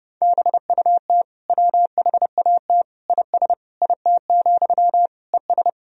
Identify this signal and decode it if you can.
Morse code, heard as a keyed tone, about 30 wpm, 700 Hz. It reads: BUT WHAT IS IT, ES
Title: Sentences from Top 300 Words and Abbreviations 30wpm